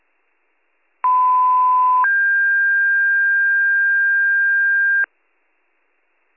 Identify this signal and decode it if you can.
Two-tone page, heard as a one-second tone, then a three-second tone, about 1010 then 1600 Hz. It is heard from 1.0 to 5.1 s.